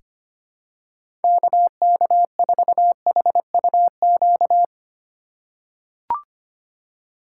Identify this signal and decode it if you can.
Morse code, heard as a keyed tone, about 25 wpm, 700 Hz. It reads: KK4HUQ
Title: USA Call Signs - 6 Characters - 2x Character Spacing - 25wpm